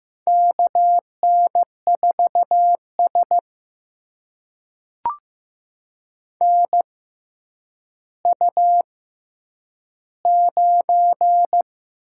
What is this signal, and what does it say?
Morse code, heard as a keyed tone, about 15 wpm, spaced as for 9 wpm, 700 Hz. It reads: KN4S NU9
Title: USA Call Signs - 4 Characters - 6x Character Spacing - 15wpm